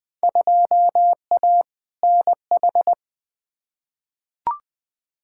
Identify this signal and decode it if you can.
Morse code, heard as a keyed tone, about 21 wpm, 700 Hz. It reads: 2A NH
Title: Field Day Exchanges - 20wpm